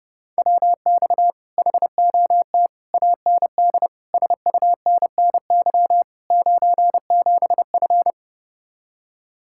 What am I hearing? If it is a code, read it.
Morse code, heard as a keyed tone, about 30 wpm, 700 Hz. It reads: WX HOT AND SUNNY 97F